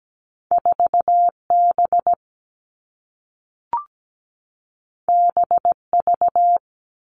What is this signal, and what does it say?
Morse code, heard as a keyed tone, about 17 wpm, 700 Hz. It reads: 4B BV
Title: All Cut Numbers - 2-Digit 17wpm